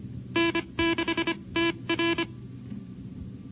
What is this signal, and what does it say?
Morse code, heard as a keyed tone, about 25 wpm, spaced as for 22 wpm, 330 Hz. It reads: N6TR
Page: NCCC SS Tape #2 - 1978 by: W1ARR, N6IG, N6NE and WA6HJV